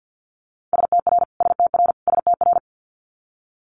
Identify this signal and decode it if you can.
Morse code, heard as a keyed tone, about 50 wpm, 690 Hz. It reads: STR STR STR
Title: Multi-letter Phonemes Repeated 3x 50wpm